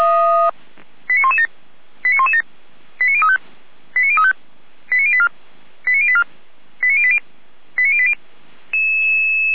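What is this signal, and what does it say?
Five-tone selective call (ZVEI), heard as quick runs of five tones, tones of about 70 ms. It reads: 89117 89117 89035 89035 89084 89084 89080 89080
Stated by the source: Five-tone ZVEI selective call received by SDR.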